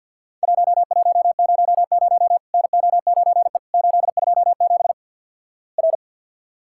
Morse code, heard as a keyed tone, about 50 wpm, 700 Hz. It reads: 1100 NO9E 827 R
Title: Worked All Europe DX Contest - QTC Exchange with Repeats - 50wpm